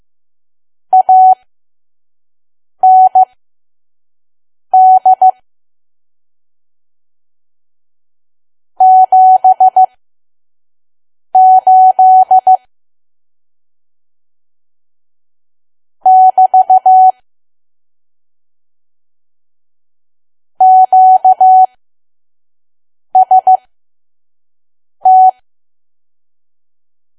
Morse code, heard as a keyed tone, about 15 wpm, spaced as for 5 wpm, 750 Hz. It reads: AND 78 <BT> QST
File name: Archive / 5 WPM files / PM